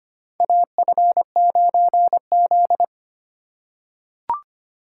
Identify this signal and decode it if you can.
Morse code, heard as a keyed tone, about 25 wpm, 700 Hz. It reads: AF9Z
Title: USA Call Signs - 4 Characters - 4x Character Spacing - 25wpm